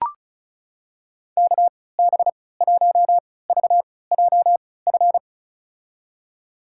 Morse code, heard as a keyed tone, about 34 wpm, spaced as for 20 wpm, 700 Hz. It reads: KB1VJF
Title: USA Call Signs - 6 Characters - 3x Character Spacing - 35wpm